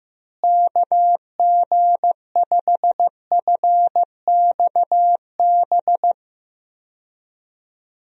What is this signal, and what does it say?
Morse code, heard as a keyed tone, about 15 wpm, 700 Hz. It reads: KG5FXB
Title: USA Call Signs - 6 Characters - Repeated 3x - 15wpm